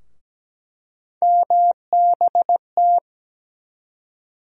Morse code, heard as a keyed tone, about 17 wpm, 700 Hz. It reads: MBT
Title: USA Call Sign Postfixes - 3 Letters 17wpm